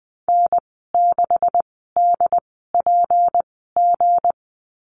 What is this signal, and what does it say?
Morse code, heard as a keyed tone, about 20 wpm, spaced as for 14 wpm, 700 Hz. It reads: N6DPG
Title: 161 Lesson 26g - Letter g - Call Signs 20wpm